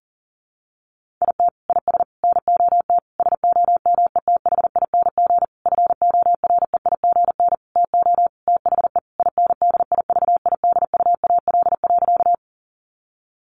Morse code, heard as a keyed tone, about 40 wpm, 700 Hz. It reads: IT IS NOT SOMETHING FOREIGN TO THE INDIVIDUAL.
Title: Sentences from Top 1400 Words 40wpm